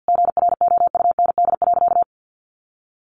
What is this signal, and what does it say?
Morse code, heard as a keyed tone, about 50 wpm, 700 Hz. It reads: GROUND.
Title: Sentences from Top 500 Words 50wpm